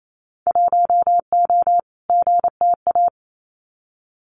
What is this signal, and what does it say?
Morse code, heard as a keyed tone, about 28 wpm, 700 Hz. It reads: 1O GTA
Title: Winter Field Day Exchanges 28wpm